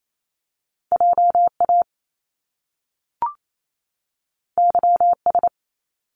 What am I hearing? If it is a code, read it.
Morse code, heard as a keyed tone, about 28 wpm, 700 Hz. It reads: JA YS
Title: Top Two Letter Combinations in Words 28wpm